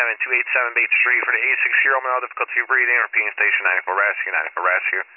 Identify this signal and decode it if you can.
Emergency Alert System (SAME) tones, heard as harsh buzzing bursts, with speech. It is heard from 1.0 to 4.5 s.